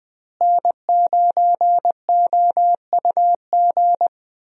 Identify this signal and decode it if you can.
Morse code, heard as a keyed tone, about 21 wpm, 700 Hz. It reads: N9OUG
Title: USA Call Signs - 5 Characters - Repeated 2x - 20wpm